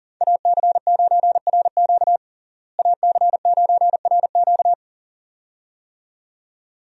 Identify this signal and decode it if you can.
Morse code, heard as a keyed tone, about 40 wpm, 700 Hz. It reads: AC9RQ AC9RQ
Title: USA Call Signs - 5 Characters - Repeated 2x - 40wpm